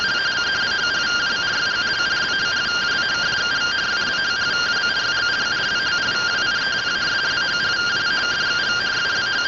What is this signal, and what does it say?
Two-tone teleprinter signal, about 170 Hz shift.
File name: findings / CT SAGITAL 27.gen-44 rtty rx